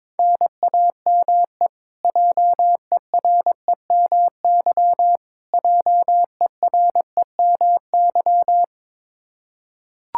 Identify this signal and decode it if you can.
Morse code, heard as a keyed tone, about 22 wpm, 700 Hz. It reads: NAME JEREMY JEREMY
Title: Practice QSO - Exchange 1 - Beginning - 22wpm